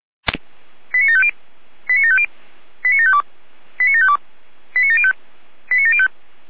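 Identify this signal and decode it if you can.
Five-tone selective call (ZVEI), heard as quick runs of five tones, tones of about 70 ms. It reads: 89750 89750 89752 89752 89705 89705
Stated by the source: ZVEI five-tone selective call received on SDR.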